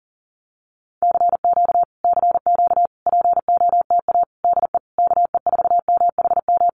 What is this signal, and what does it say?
Morse code, heard as a keyed tone, about 40 wpm, 700 Hz. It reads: CQ CQ POTA DE KE4MHM
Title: Common QSO Phrases Encore 40wpm